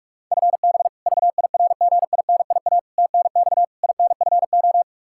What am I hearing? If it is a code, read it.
Morse code, heard as a keyed tone, about 42 wpm, 700 Hz. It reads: FB VIRGINIA TNX INFO
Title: K1USN SST QSO Practice 45wpm